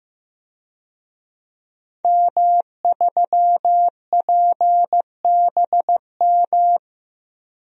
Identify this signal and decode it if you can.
Morse code, heard as a keyed tone, about 15 wpm, 700 Hz. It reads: M3PBM
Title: DXCC-100 Call Signs - 5 Characters - Repeated 2x - 15wpm